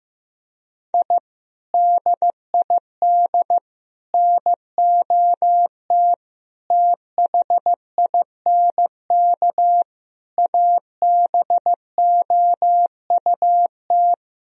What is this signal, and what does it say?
Morse code, heard as a keyed tone, about 15 wpm, 700 Hz. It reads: I DID NOT THINK ABOUT
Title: Sentences from Top 1000 Words - Review Part 4 - 15wpm